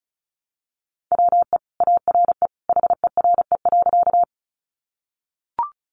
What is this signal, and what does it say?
Morse code, heard as a keyed tone, about 34 wpm, 700 Hz. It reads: WE ARE HERE.
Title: Sentences from Top 200 Words 35wpm